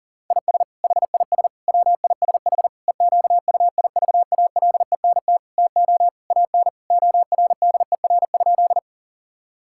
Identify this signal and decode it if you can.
Morse code, heard as a keyed tone, about 40 wpm, 700 Hz. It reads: IS HIS WISH EQUIVALENT TO AN ORDER?
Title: Sentences from Top 1600 Words 40wpm